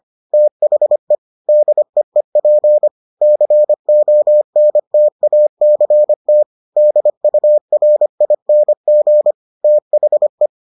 Morse code, heard as a keyed tone, about 25 wpm, 600 Hz. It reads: THE DEEP CONTACT DURING THE